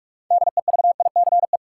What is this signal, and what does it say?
Morse code, heard as a keyed tone, about 46 wpm, 700 Hz. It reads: DEVICE
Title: Sentences from Top 1500 Words - New Word Drill - 45wpm